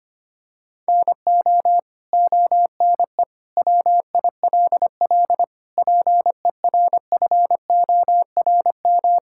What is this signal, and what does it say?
Morse code, heard as a keyed tone, about 25 wpm, 700 Hz. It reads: NO ONE WILL PERFORM
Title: Sentences from Top 1000 Words - Review Part 5 - 25wpm